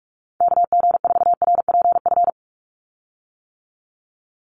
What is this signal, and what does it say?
Morse code, heard as a keyed tone, about 44 wpm, 700 Hz. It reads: KG4RPF
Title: USA Call Signs 45wpm